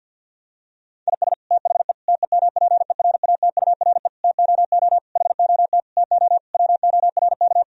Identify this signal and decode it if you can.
Morse code, heard as a keyed tone, about 51 wpm, 700 Hz. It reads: IS THE TEMPERATURE TOO HOT TO WORK
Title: Sentences from Top 1000 Words - Review Part 4 - 50wpm